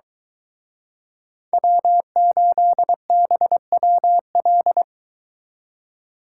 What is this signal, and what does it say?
Morse code, heard as a keyed tone, about 22 wpm, 700 Hz. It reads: W8BWL